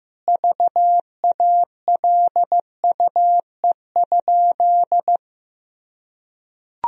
Morse code, heard as a keyed tone, about 15 wpm, 700 Hz. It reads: VALUE?
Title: Sentences from Top 1000 Words - Review Part 3 - 15wpm